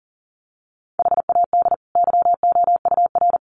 Morse code, heard as a keyed tone, about 40 wpm, 700 Hz. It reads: HAD YOUR
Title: Sentences from Top 1500 Words 40wpm